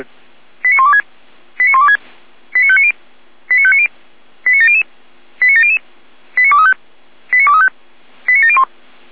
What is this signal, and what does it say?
Five-tone selective call (ZVEI), heard as quick runs of five tones, tones of about 70 ms. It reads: 89126 89126 89590 89590 89700 89700 89235 89235 89701